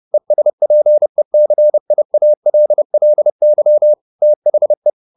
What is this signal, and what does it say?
Morse code, heard as a keyed tone, about 30 wpm, 600 Hz. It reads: ESPECIALLY THE
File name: the old man and the sea / the old man and the sea 1 30wpm